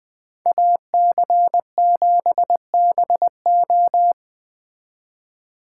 Morse code, heard as a keyed tone, about 24 wpm, 700 Hz. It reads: AC7BO
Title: Speed Racing - USA 2x2 Call Signs 20wpm